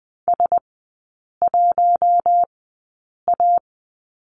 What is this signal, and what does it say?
Morse code, heard as a keyed tone, about 20 wpm, spaced as for 8 wpm, 700 Hz. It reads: S1A
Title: 041 Lesson 8e - Number 1 - Three Characters 20wpm